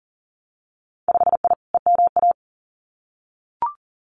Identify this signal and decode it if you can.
Morse code, heard as a keyed tone, about 38 wpm, 700 Hz. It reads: 5I EMA
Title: Winter Field Day Exchanges Encore 40wpm